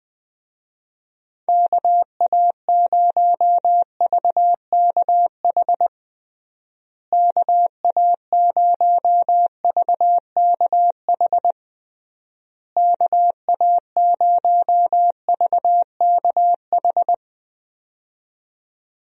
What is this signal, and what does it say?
Morse code, heard as a keyed tone, about 20 wpm, 700 Hz. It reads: KA0VKH KA0VKH KA0VKH
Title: USA Call Signs - 6 Characters - Repeated 3x - 20wpm